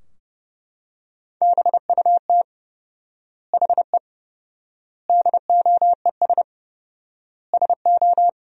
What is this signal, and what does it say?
Morse code, heard as a keyed tone, about 30 wpm, 700 Hz. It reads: BUT HE DOES SO
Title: 4x Word-Spacing - Sentences from Top 200 Words - 30wpm